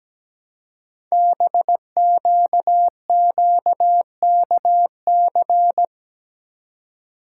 Morse code, heard as a keyed tone, about 17 wpm, 700 Hz. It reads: BQQKC
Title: Code Groups - 5 Letters - 17wpm